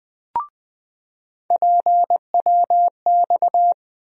Morse code, heard as a keyed tone, about 20 wpm, 700 Hz. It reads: PWX
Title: USA Call Sign Postfixes - 3 Letters 20wpm